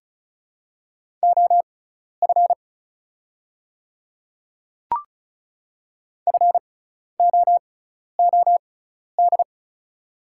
Morse code, heard as a keyed tone, about 36 wpm, spaced as for 12 wpm, 700 Hz. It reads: OF FOOD
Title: 6x Character Spacing - Top 300 Words - 35wpm